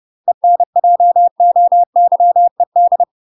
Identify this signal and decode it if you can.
Morse code, heard as a keyed tone, about 30 wpm, 700 Hz. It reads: ENJOYED
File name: UK MASTER QSOs TRAINING / 30WPM / UK MASTER QSOs 0001